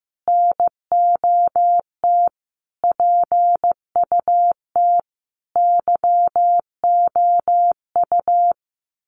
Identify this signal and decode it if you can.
Morse code, heard as a keyed tone, about 15 wpm, 700 Hz. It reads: NOT PUT YOU
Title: Sentences from Top 1100 Words 15wpm